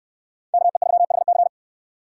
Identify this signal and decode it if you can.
Morse code, heard as a keyed tone, about 69 wpm, 690 Hz. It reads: KE3HC